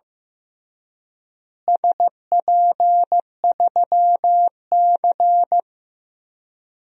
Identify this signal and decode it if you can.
Morse code, heard as a keyed tone, about 15 wpm, 700 Hz. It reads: SP3C
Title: DXCC-100 Call Signs - 4 Characters - Repeated 3x - 15wpm